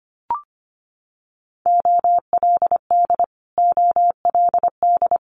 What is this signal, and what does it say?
Morse code, heard as a keyed tone, about 25 wpm, 700 Hz. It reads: OLD OLD
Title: Top 300 Words Repeated 25wpm